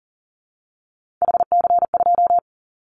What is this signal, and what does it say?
Morse code, heard as a keyed tone, about 40 wpm, 700 Hz. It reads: HC2
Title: Callsign Prefixes for Top 100 DXCC - 40wpm